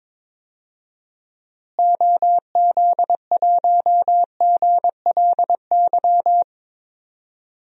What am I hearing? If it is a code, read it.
Morse code, heard as a keyed tone, about 21 wpm, 700 Hz. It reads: OZ1GLY
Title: DXCC-100 Call Signs - 6 Characters - Repeated 2x - 22wpm